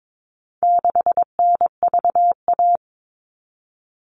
Morse code, heard as a keyed tone, about 22 wpm, 700 Hz. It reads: 6NVA
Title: All Cut Numbers - 4-Digit 22wpm